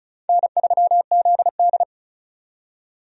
Morse code, heard as a keyed tone, about 36 wpm, 700 Hz. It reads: N3ZD
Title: USA Call Signs - 4 Characters - 35wpm